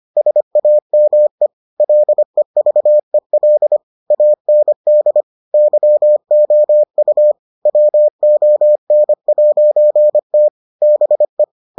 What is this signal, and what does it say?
Morse code, heard as a keyed tone, about 25 wpm, 600 Hz. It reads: SAME LEVEL AND YOU WON'T BE